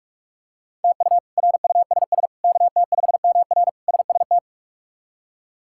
Morse code, heard as a keyed tone, about 46 wpm, 700 Hz. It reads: TU RUSS KT5MR SST
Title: K1USN SST QSO Practice - 45wpm